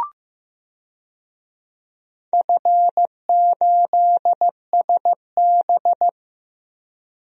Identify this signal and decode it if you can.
Morse code, heard as a keyed tone, about 15 wpm, 700 Hz. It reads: F8SB
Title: DXCC-100 Call Signs - 4 Characters - 15wpm